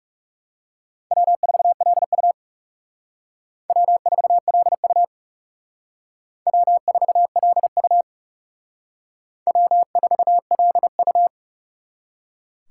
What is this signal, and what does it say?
Morse code, heard as a keyed tone, about 40 wpm, 700 Hz. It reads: W4LU W4LU W4LU W4LU